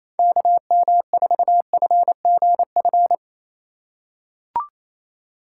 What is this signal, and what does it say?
Morse code, heard as a keyed tone, about 28 wpm, 700 Hz. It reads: KM4FGF
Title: USA Call Signs - 6 Characters - 3x Character Spacing - 28wpm